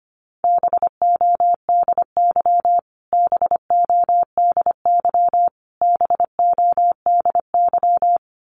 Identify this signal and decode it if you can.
Morse code, heard as a keyed tone, about 25 wpm, 700 Hz. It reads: BODY BODY BODY
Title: Top 500 Words Repeated 25wpm